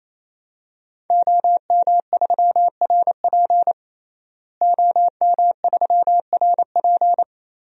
Morse code, heard as a keyed tone, about 28 wpm, 700 Hz. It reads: OM3RP OM3RP
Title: DXCC-100 Call Signs - 5 Characters - Repeated 3x - 28wpm